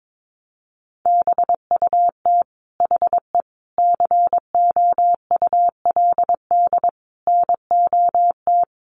Morse code, heard as a keyed tone, about 22 wpm, 700 Hz. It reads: BUT HE COULD NOT
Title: Sentences from Top 600 Words 22wpm